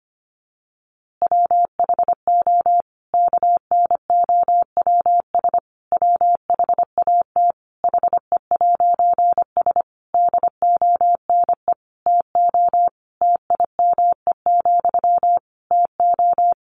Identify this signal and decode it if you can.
Morse code, heard as a keyed tone, about 25 wpm, 700 Hz. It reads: WHO KNOWS WHAT HE'S DONE TO TIME, TO
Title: A Sound Of Thunder 25wpm